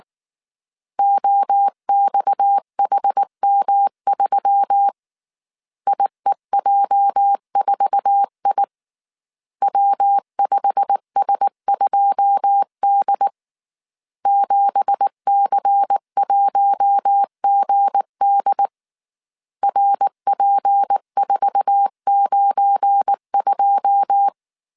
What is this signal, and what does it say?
Morse code, heard as a keyed tone, about 19 wpm, 800 Hz. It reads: OXHM3 IEJ4I W5S2D 7C1GD RP492
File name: CW / DET / 1 9 1500C S A 18wpm 2459